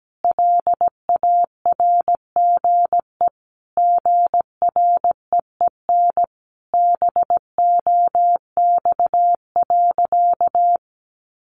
Morse code, heard as a keyed tone, about 17 wpm, 700 Hz. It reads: LARGE GREEN BOX.
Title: Sentences from Top 500 Words 17wpm